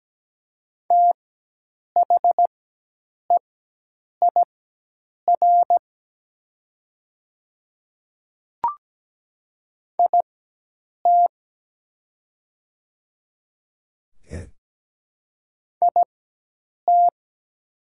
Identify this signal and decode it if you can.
Morse code, heard as a keyed tone, about 17 wpm, spaced as for 8 wpm, 700 Hz. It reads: THEIR IT IT
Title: 4x Character Spacing - Top 100 Words - 17wpm